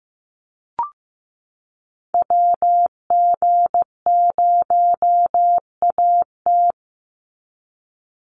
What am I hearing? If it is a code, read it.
Morse code, heard as a keyed tone, about 15 wpm, 700 Hz. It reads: WG0AT
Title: NA SOTA Chaser Callsigns 15wpm